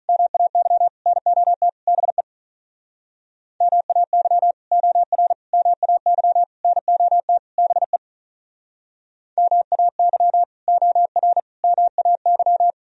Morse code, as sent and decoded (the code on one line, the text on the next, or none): -- .- -.-- / -. --- - / -... / -- .- -.-- / --- .-. / -- .- -.-- / -. --- - / -... / -- .- -.-- / --- .-. / -- .- -.--
MAY NOT BE MAY OR MAY NOT BE MAY OR MAY